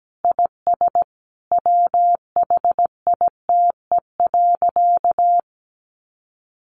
Morse code, heard as a keyed tone, about 17 wpm, 700 Hz. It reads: IS WHITE.
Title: Easy Phrases 17wpm